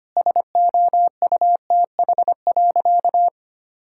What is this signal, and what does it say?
Morse code, heard as a keyed tone, about 25 wpm, 700 Hz. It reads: SOUTH.
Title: Sentences from Top 1000 Words - Review Part 6 - 25wpm